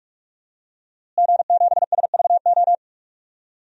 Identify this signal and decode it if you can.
Morse code, heard as a keyed tone, about 45 wpm, 700 Hz. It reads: G7SVO
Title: DXCC-100 Call Signs - 5 Characters - Repeated 3x - 45wpm